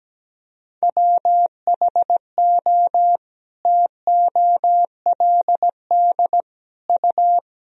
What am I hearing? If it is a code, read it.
Morse code, heard as a keyed tone, about 17 wpm, 700 Hz. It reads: WHO TOLD U
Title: Sentences from Top 400 Words and Abbreviations 17wpm